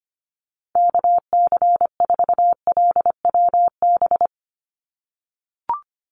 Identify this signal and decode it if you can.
Morse code, heard as a keyed tone, about 25 wpm, 690 Hz. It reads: KC4LWB
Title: USA Call Signs 25wpm